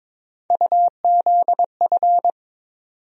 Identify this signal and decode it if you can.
Morse code, heard as a keyed tone, about 22 wpm, 700 Hz. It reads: UZF
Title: USA Call Sign Postfixes - 3 Letters 22wpm